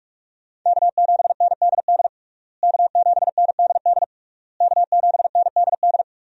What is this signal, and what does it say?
Morse code, heard as a keyed tone, about 46 wpm, 700 Hz. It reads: K7NDD K7NDD K7NDD
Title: USA Call Signs - 5 Characters - Repeated 3x - 45wpm